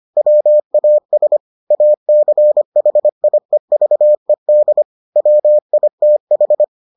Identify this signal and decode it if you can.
Morse code, heard as a keyed tone, about 25 wpm, 600 Hz. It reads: WAS ACHIEVED WITH